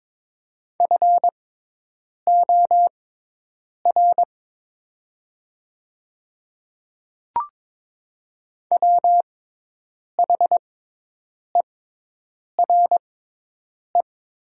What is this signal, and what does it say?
Morse code, heard as a keyed tone, about 22 wpm, spaced as for 8 wpm, 700 Hz. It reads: FOR WHERE
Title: 6x Character Spacing - Top 200 Words - 22wpm